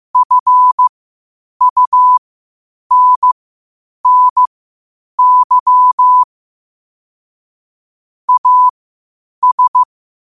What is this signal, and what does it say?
Morse code, heard as a keyed tone, about 15 wpm, spaced as for 8 wpm, 1000 Hz. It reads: FUNNY AS
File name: qotdfiles / 8wpm